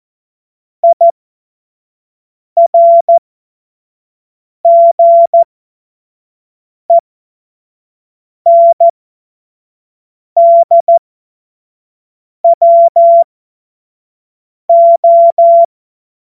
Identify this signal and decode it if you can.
Morse code, heard as a keyed tone, about 14 wpm, spaced as for 5 wpm, 680 Hz. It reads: IRGENDWO
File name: Morsekurs / Witze / Witz 06. April 2021, 25 F70